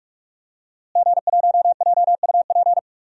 Morse code, heard as a keyed tone, about 46 wpm, 700 Hz. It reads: G1JUP
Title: DXCC-100 Call Signs - 5 Characters - Repeated 3x - 45wpm